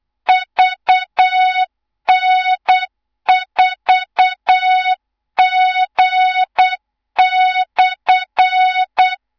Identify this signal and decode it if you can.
Morse code, heard as a keyed tone, about 8 wpm, 3700 Hz, with noise. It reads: VN4G/
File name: Morze-HA2MN / hang / 24 GYAKORLAT